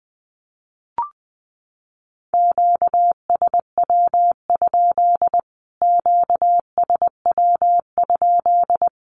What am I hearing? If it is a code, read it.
Morse code, heard as a keyed tone, about 20 wpm, 700 Hz. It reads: QSW? QSW?
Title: Q-Codes Repeated 3x 20wpm